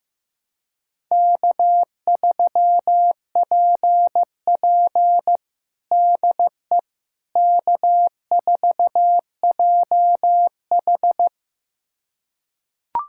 Morse code, heard as a keyed tone, about 15 wpm, 700 Hz. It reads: K3PP DE K4JH
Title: USA Callsign DE Callsign - 4 Characters - 15wpm